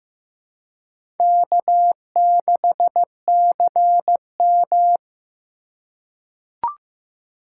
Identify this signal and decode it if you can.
Morse code, heard as a keyed tone, about 15 wpm, 700 Hz. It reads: K6CM
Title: USA Call Signs - 4 Characters - 4x Character Spacing - 15wpm